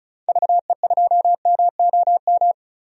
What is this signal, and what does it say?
Morse code, heard as a keyed tone, about 36 wpm, 700 Hz. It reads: VE2MOM
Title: DXCC-100 Call Signs - 6 Characters - 35wpm